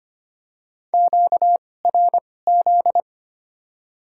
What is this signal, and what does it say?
Morse code, heard as a keyed tone, about 25 wpm, spaced as for 18 wpm, 700 Hz. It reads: QRZ
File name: Top 90 QSO Elements - 2x Character Spacing - 25wpm